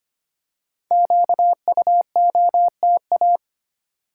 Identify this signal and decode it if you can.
Morse code, heard as a keyed tone, about 25 wpm, 700 Hz. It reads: QUOTA
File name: 5 Letter Words - First 3 Letters Given - 25wpm